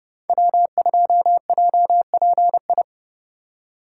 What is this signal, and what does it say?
Morse code, heard as a keyed tone, about 35 wpm, 700 Hz. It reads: W2JPI
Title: Speed Racing - USA 1x3 Call Signs 30wpm